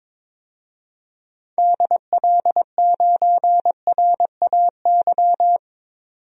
Morse code, heard as a keyed tone, about 22 wpm, 700 Hz. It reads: DL9RAY
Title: DXCC-100 Call Signs - 6 Characters - 22wpm